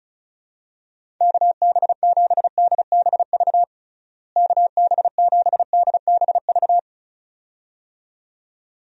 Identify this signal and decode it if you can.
Morse code, heard as a keyed tone, about 34 wpm, 700 Hz. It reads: KB7DBV KB7DBV
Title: USA Call Signs - 6 Characters - Repeated 2x - 35wpm